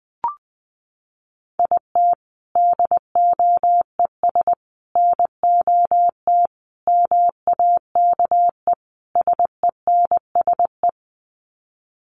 Morse code, heard as a keyed tone, about 20 wpm, 700 Hz. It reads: IT DOES NOT MAKE SENSE
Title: Sets of 5 Words Encore2 20wpm